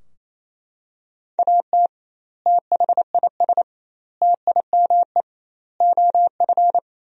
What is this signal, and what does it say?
Morse code, heard as a keyed tone, about 28 wpm, 700 Hz. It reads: AT THIS TIME OF
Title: 2x Word-Spacing - Sentences from Top 300 Words - 28wpm